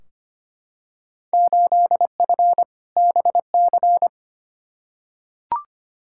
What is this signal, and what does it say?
Morse code, heard as a keyed tone, about 25 wpm, 700 Hz. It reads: 8F BC
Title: Field Day Exchanges Encore - 25wpm